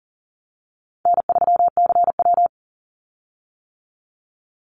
Morse code, heard as a keyed tone, about 41 wpm, 700 Hz. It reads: N3CW
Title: Easy Call Signs - 40wpm